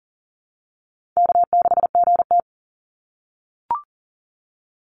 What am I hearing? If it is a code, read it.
Morse code, heard as a keyed tone, about 40 wpm, 700 Hz. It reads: K6GT
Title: Call Signs - 40wpm